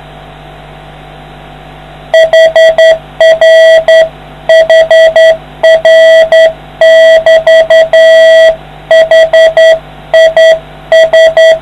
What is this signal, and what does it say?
Morse code, heard as a keyed tone, about 9 wpm, 650 Hz. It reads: HR HR <BT> HIS